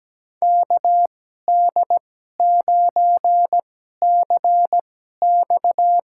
Morse code, heard as a keyed tone, about 17 wpm, spaced as for 12 wpm, 700 Hz. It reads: KD9CX
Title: USA Call Signs - 2x Character Spacing - 17wpm